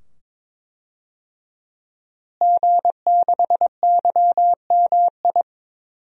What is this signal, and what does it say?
Morse code, heard as a keyed tone, about 22 wpm, 700 Hz. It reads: G6YMI
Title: DXCC-100 Call Signs - 5 Characters - Repeated 2x - 22wpm